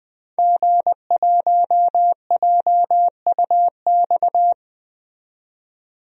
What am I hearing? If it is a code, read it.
Morse code, heard as a keyed tone, about 21 wpm, 700 Hz. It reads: G1JUX
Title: DXCC-100 Call Signs - 5 Characters - 20wpm